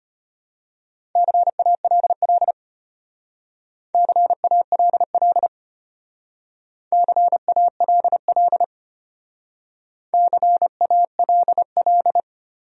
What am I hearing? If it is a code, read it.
Morse code, heard as a keyed tone, about 31 wpm, 700 Hz. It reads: CALL CALL CALL CALL